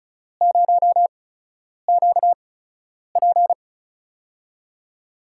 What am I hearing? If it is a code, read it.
Morse code, heard as a keyed tone, about 36 wpm, spaced as for 10 wpm, 700 Hz. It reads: 0QP
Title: USA Call Signs - 4 Characters - 8x Character Spacing - 35wpm